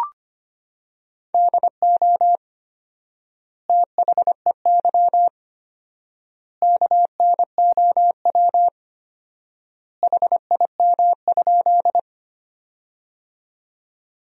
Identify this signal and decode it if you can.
Morse code, heard as a keyed tone, about 25 wpm, 700 Hz. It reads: DO THEY KNOW HIM?
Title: 4x Word-Spacing - Sentences from Top 100 Words - 25wpm